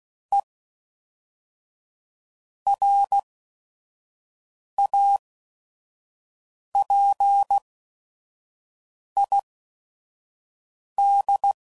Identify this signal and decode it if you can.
Morse code, heard as a keyed tone, about 16 wpm, spaced as for 5 wpm, 800 Hz. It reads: E RAPID
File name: CW Code Groups @5WPM / Technical Subjects / XC